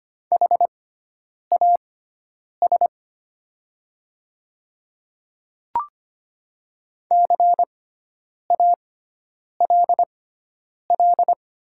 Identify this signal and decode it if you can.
Morse code, heard as a keyed tone, about 25 wpm, spaced as for 9 wpm, 700 Hz. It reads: HAS CALL
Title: 6x Character Spacing - Top 100 Words - 25wpm